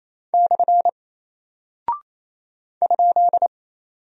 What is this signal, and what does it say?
Morse code, heard as a keyed tone, about 28 wpm, 700 Hz. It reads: / ?
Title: Single Character (Letter:Number:Punctuation:Prosign) Alt 28wpm